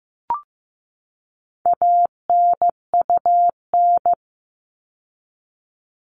Morse code, heard as a keyed tone, about 15 wpm, 700 Hz. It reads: ANUN
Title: All Cut Numbers - 4-Digit 15wpm